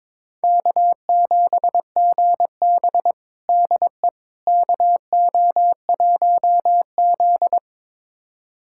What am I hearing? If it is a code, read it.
Morse code, heard as a keyed tone, about 21 wpm, 700 Hz. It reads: K7GB DE KO1Z
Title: USA Callsign DE Callsign - 4 Characters - 22wpm